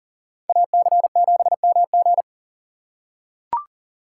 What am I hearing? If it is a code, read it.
Morse code, heard as a keyed tone, about 40 wpm, 700 Hz. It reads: AC7MG
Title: USA Call Signs - 5 Characters - 4x Character Spacing - 40wpm